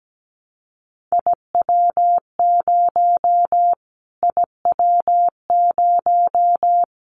Morse code, heard as a keyed tone, about 17 wpm, 700 Hz. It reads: IW0 IW0
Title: Callsign Prefixes for Top 25 DXCC - Repeated 3x - 17wpm